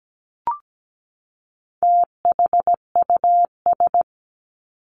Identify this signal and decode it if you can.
Morse code, heard as a keyed tone, about 17 wpm, 700 Hz. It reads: THUS
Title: Top 751-1000 Words 17wpm